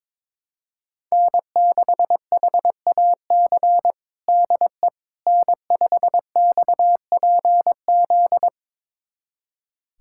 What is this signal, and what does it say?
Morse code, heard as a keyed tone, about 22 wpm, 700 Hz. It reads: N6HAC DE N5XPZ
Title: USA Callsign DE Callsign - 5 Characters - 22wpm